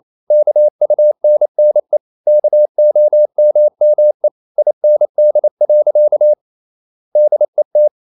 Morse code, heard as a keyed tone, about 28 wpm, 600 Hz. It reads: KUNNE KOMME IND. DET
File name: Ferie QRQ 28wpm